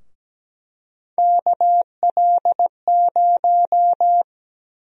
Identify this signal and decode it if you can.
Morse code, heard as a keyed tone, about 17 wpm, 700 Hz. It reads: KL0
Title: USA Call Sign Prefixes - 17wpm